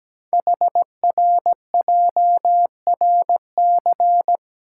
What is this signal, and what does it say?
Morse code, heard as a keyed tone, about 17 wpm, 700 Hz. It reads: HRJRC
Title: Code Groups - 5 Letters - 17wpm